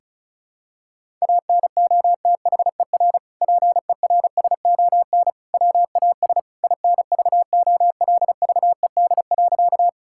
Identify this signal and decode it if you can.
Morse code, heard as a keyed tone, about 34 wpm, 700 Hz. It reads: ANOTHER PERSON WAS INVOLVED.
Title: Sentences from Top 1000 Words - Review Part 5 - 35wpm